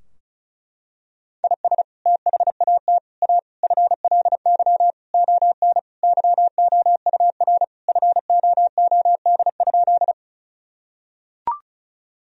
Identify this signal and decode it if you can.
Morse code, heard as a keyed tone, about 36 wpm, 700 Hz. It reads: IS THAT A FLY ON YOUR FOOD?
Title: Sentences from Top 1000 Words - Review Part 2 - 35wpm